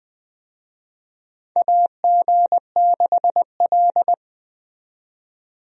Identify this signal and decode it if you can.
Morse code, heard as a keyed tone, about 21 wpm, 700 Hz. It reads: AG6L
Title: USA Call Signs - 4 Characters - 20wpm